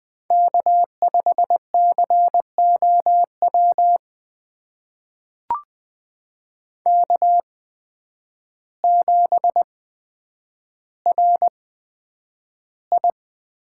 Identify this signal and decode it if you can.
Morse code, heard as a keyed tone, about 20 wpm, spaced as for 9 wpm, 700 Hz. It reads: K5COW K7RI
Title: USA Call Signs - 5 Characters - 8x Character Spacing - 20wpm